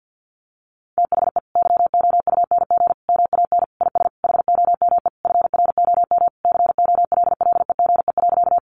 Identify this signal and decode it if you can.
Morse code, heard as a keyed tone, about 50 wpm, 700 Hz. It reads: THE YOUNG MAN IS HOME FROM COLLEGE.
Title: Sentences from Top 1300 Words 50wpm